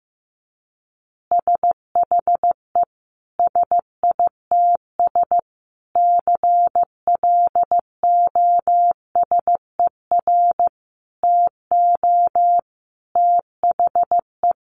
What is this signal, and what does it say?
Morse code, heard as a keyed tone, about 15 wpm, 700 Hz. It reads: SHE SITS CLOSER TO THE
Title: Additional Easy Phrases 15wpm